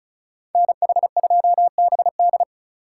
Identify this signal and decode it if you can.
Morse code, heard as a keyed tone, about 36 wpm, 700 Hz. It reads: NH2BD
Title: USA Call Signs - 5 Characters - 2x Character Spacing - 35wpm